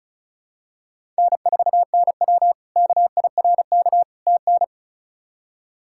Morse code, heard as a keyed tone, about 36 wpm, 700 Hz. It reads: N4NW KIRK TN
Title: K1USN SST QSO Practice - 35wpm